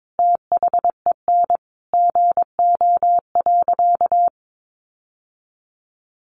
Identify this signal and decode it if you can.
Morse code, heard as a keyed tone, about 22 wpm, 700 Hz. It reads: THEN GO.
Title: Sentences from Top 1300 Words 22wpm